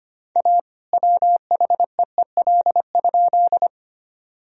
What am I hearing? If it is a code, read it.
Morse code, heard as a keyed tone, about 25 wpm, 700 Hz. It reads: A WHEEL?
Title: Sentences from Top 1000 Words - Review Part 3 - 25wpm